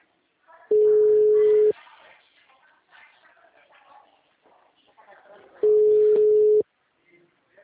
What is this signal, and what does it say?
Ringback tone (Europe, 425 Hz), with speech.